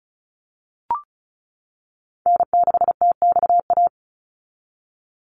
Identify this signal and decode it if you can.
Morse code, heard as a keyed tone, about 34 wpm, 700 Hz. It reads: N6TXA
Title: USA Call Signs 35wpm